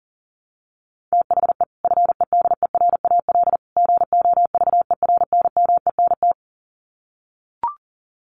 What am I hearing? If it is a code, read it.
Morse code, heard as a keyed tone, about 40 wpm, 700 Hz. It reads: THE FEDERAL GOVERNMENT
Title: Sets of 3 Words 40wpm